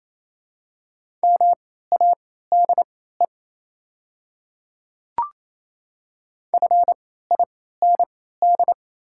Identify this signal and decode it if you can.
Morse code, heard as a keyed tone, about 28 wpm, spaced as for 16 wpm, 700 Hz. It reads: MADE FIND
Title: 3x Character Spacing - Top 200 Words - 28wpm